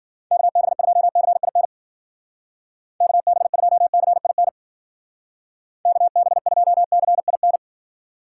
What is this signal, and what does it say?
Morse code, heard as a keyed tone, about 48 wpm, 700 Hz. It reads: KB2CIN KB2CIN KB2CIN
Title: Speed Racing - USA 2x3 Call Signs 40wpm